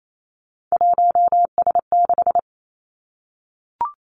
Morse code, heard as a keyed tone, about 28 wpm, 690 Hz. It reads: 1S6
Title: Three Challenging Letter-Number Combinations 28wpm